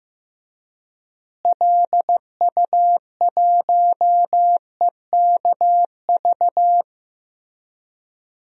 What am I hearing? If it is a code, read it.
Morse code, heard as a keyed tone, about 15 wpm, 700 Hz. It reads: LU1EKV
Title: DXCC-100 Call Signs - 6 Characters - Repeated 3x - 15wpm